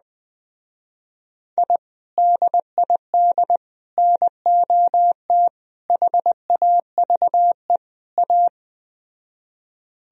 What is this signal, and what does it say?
Morse code, heard as a keyed tone, about 20 wpm, 700 Hz. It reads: I DID NOT HAVE A
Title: Speed-Racing - Sets of 5 Words 17wpm